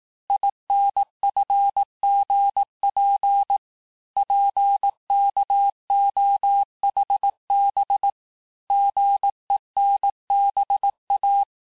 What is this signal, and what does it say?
Morse code, heard as a keyed tone, about 18 wpm, 800 Hz. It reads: INFGP PKOHB GENBA
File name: cours-du-lundi files / cours-1594--teste-chiffre-vitesse-17mpm